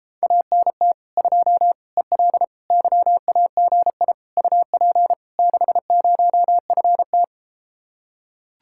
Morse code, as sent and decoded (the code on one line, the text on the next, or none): .- -. - / ..--- / . .-.. / -.-- .- --. .. / ..- .--. / -.... ----- ..-. -
ANT 2 EL YAGI UP 60FT